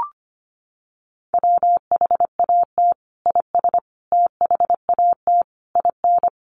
Morse code, heard as a keyed tone, about 25 wpm, 700 Hz. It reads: WHAT IS THAT IN
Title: Sentences from Top 200 Words 25wpm